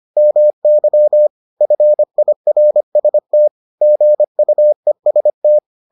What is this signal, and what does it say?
Morse code, heard as a keyed tone, about 25 wpm, 600 Hz. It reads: MY FIRST GUEST